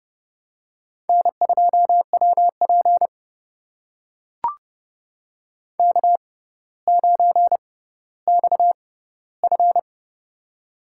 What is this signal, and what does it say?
Morse code, heard as a keyed tone, about 31 wpm, spaced as for 15 wpm, 700 Hz. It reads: N2WP K9XF
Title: USA Call Signs - 4 Characters - 6x Character Spacing - 30wpm